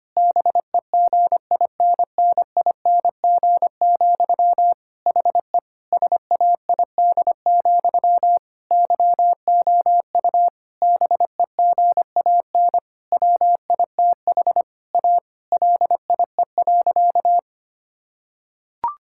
Morse code, heard as a keyed tone, about 25 wpm, 700 Hz. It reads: BEGINNING, HE SAID, YOU BEGAN WITH A LIE.
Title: Sentences from Top 1600 Words 25wpm